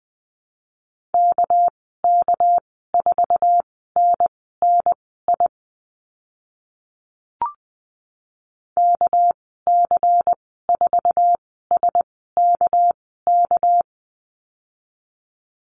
Morse code, heard as a keyed tone, about 19 wpm, spaced as for 14 wpm, 700 Hz. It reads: KK4NNI KC4SKK
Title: 200 Lesson 32g - Letter k - Call Signs 20wpm